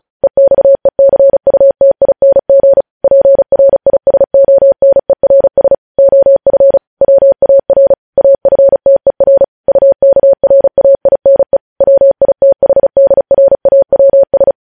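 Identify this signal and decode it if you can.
Morse code, heard as a keyed tone, about 36 wpm, 550 Hz. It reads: EXECUTING PRISONERS OF WAR AFTER UKRAINE WITHDRAWS
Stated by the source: Morse code news headlines sent at 35 words per minute.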